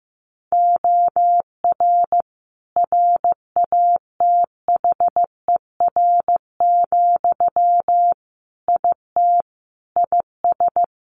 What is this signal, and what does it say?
Morse code, heard as a keyed tone, about 15 wpm, 700 Hz. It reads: OR RATHER, IT IS
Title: Sentences from Top 1400 Words 15wpm